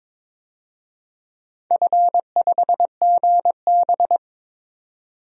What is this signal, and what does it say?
Morse code, heard as a keyed tone, about 22 wpm, 700 Hz. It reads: F5GB
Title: DXCC-100 Call Signs - 4 Characters - 22wpm